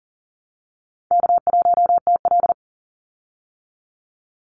Morse code, heard as a keyed tone, about 39 wpm, 700 Hz. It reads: K1TL
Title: Easy Call Signs Repeated 2x - 40wpm